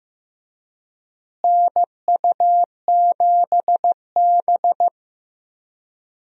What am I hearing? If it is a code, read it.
Morse code, heard as a keyed tone, about 15 wpm, 700 Hz. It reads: NU7B
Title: USA Call Signs - 4 Characters - Repeated 3x - 15wpm